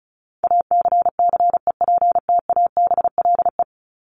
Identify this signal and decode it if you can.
Morse code, heard as a keyed tone, about 34 wpm, 700 Hz. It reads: ACCEPTABLE
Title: Top 1501-1750 Words 35wpm